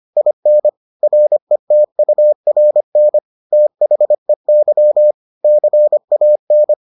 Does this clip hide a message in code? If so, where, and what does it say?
Yes: Morse code from 0.2 to 6.7 s, heard as a keyed tone, about 25 wpm, 600 Hz. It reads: IN RETURN THEY CAN